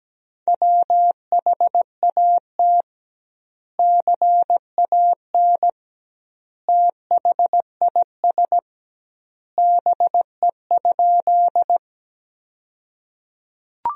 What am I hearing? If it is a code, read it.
Morse code, heard as a keyed tone, about 17 wpm, 700 Hz. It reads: WHAT CAN THIS BE?
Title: 2x Word-Spacing - Sentences from Top 100 Words - 17wpm